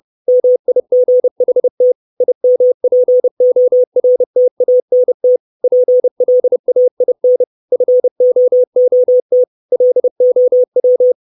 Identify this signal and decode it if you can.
Morse code, heard as a keyed tone, about 30 wpm, 500 Hz. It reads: MIGHT IMPORTANT PLAIN FOOT LOW
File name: 632 Words 30WPM